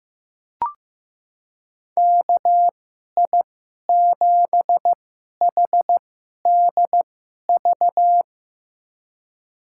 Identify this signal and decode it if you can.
Morse code, heard as a keyed tone, about 15 wpm, spaced as for 11 wpm, 700 Hz. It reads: KI7HDV
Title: USA Call Signs - 2x Character Spacing - 15wpm